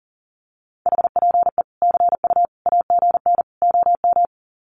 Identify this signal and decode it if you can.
Morse code, heard as a keyed tone, about 40 wpm, 700 Hz. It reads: HPE CU AGN OM
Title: Common QSO Phrases 40wpm